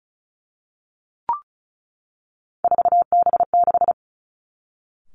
Morse code, heard as a keyed tone, about 36 wpm, 700 Hz. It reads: 4B6
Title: Three Challenging Letter-Number Combinations 35wpm